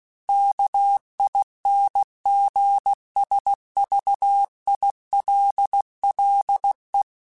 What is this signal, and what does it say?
Morse code, heard as a keyed tone, about 16 wpm, 800 Hz. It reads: KINGSVILLE
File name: fromSMs / FromSMs mp3s / Cities@15